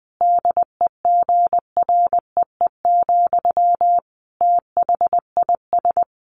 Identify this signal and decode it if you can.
Morse code, heard as a keyed tone, about 20 wpm, 700 Hz. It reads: DEGREE, THIS
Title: Sentences from Top 1500 Words 20wpm